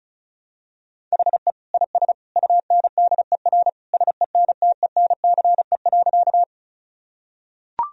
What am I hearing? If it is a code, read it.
Morse code, heard as a keyed tone, about 36 wpm, 700 Hz. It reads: HE IS UNDER SENTENCE.
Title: Sentences from Top 1000 Words - Review Part 1 - 35wpm